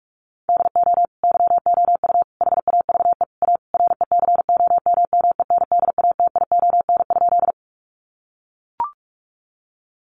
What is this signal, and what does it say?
Morse code, heard as a keyed tone, about 44 wpm, 700 Hz. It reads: DO YOU HAVE A RECOMMENDATION?
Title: Sentences from Top 1400 Words 45wpm